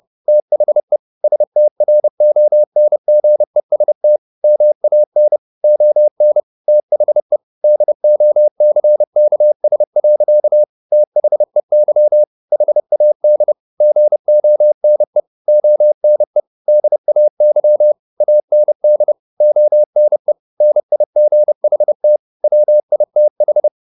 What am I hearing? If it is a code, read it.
Morse code, heard as a keyed tone, about 30 wpm, 600 Hz. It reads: THE STRONGEST MAN ON THE DOCKS. THEY HAD GONE ONE DAY AND ONE NIGHT WITH
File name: the old man and the sea / the old man and the sea 1 30wpm